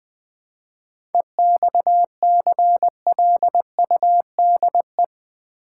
Morse code, heard as a keyed tone, about 20 wpm, 700 Hz. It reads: EXCLUDE
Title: 7 Letter Words - First 3 Letters Given - 20wpm